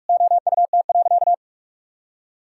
Morse code, heard as a keyed tone, about 44 wpm, 700 Hz. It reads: OUT.
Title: Sentences from Top 1000 Words - Review Part 6 - 45wpm